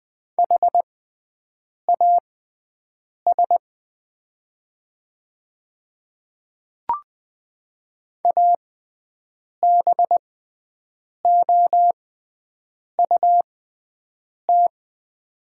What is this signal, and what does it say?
Morse code, heard as a keyed tone, about 20 wpm, spaced as for 7 wpm, 700 Hz. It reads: HAS ABOUT
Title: 6x Character Spacing - Top 100 Words - 20wpm